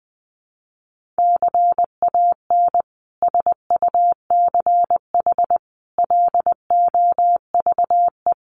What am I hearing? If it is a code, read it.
Morse code, heard as a keyed tone, about 19 wpm, 700 Hz. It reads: CAN SUCH LOVE
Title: Sentences from Top 1100 Words 20wpm